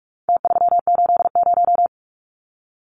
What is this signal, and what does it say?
Morse code, heard as a keyed tone, about 44 wpm, 700 Hz. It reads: T380
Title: Numbers - Contest Style - 4-Digit 45wpm